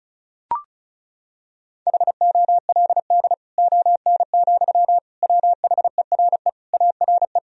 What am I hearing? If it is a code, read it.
Morse code, heard as a keyed tone, about 34 wpm, 700 Hz. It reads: HOLD ON, WHERE ARE
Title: Sentences from Top 1000 Words - Review Part 2 - 35wpm